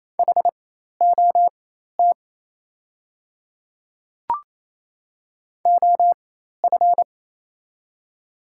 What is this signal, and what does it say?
Morse code, heard as a keyed tone, about 28 wpm, spaced as for 13 wpm, 700 Hz. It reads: HOT OF
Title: 4x Character Spacing - Top 100 Words - 28wpm